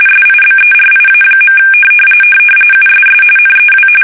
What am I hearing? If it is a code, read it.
RTTY, about 45.45 baud. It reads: SIBERIA. USS NEW
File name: fsk850 / battleshipnj